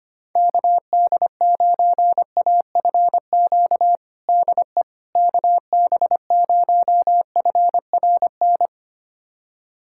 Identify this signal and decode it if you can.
Morse code, heard as a keyed tone, about 25 wpm, 700 Hz. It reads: KD9AFQ DE KB0FRN
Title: USA Callsign DE Callsign - 6 Characters - 25wpm